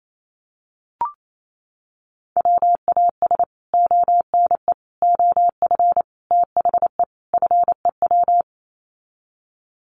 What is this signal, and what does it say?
Morse code, heard as a keyed tone, about 28 wpm, 700 Hz. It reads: WAS ONE OF THE FEW
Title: Sets of 5 Words Encore 28wpm